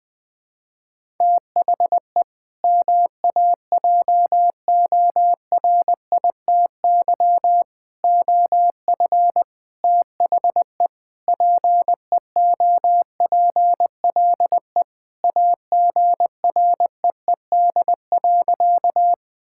Morse code, heard as a keyed tone, about 20 wpm, 700 Hz. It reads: THE MAJORITY OF THE PEOPLE AGREED.
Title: Sentences with 4-Word Phrases - 20wpm